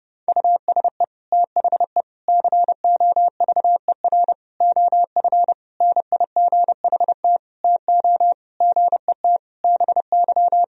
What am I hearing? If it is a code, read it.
Morse code, heard as a keyed tone, about 30 wpm, 700 Hz. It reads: USE THE COVER OF NIGHT TO GET BY
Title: Sentences from Top 1000 Words - Review Part 1 - 30wpm